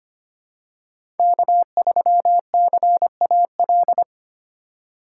Morse code, heard as a keyed tone, about 25 wpm, 700 Hz. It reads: K3CAL
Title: USA Call Signs - 3x Character Spacing - 25wpm